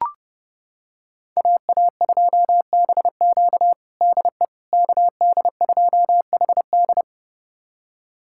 Morse code, heard as a keyed tone, about 31 wpm, 700 Hz. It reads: AA2BQ DE KD2HD
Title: USA Callsign DE Callsign - 5 Characters - 30wpm